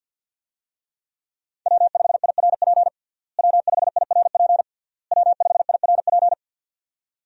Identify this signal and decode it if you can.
Morse code, heard as a keyed tone, about 52 wpm, 700 Hz. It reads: W5IRP W5IRP W5IRP
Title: USA Call Signs - 5 Characters - Repeated 3x - 50wpm